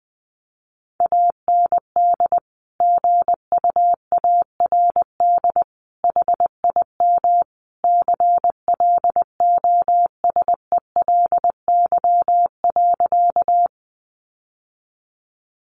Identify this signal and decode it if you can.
Morse code, heard as a keyed tone, about 20 wpm, 700 Hz. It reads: AND GUARD HIM CLOSELY.
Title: Sentences from Top 1400 Words 20wpm